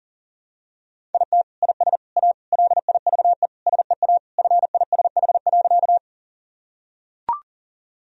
Morse code, heard as a keyed tone, about 40 wpm, 700 Hz. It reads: IT IS A LIVE SEA FISH.
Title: Sentences from Top 1000 Words - Review Part 1 - 40wpm